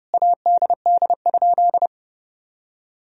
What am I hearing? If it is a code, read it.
Morse code, heard as a keyed tone, about 30 wpm, 700 Hz. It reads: ADD?
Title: Sentences from Top 1000 Words - Review Part 1 - 30wpm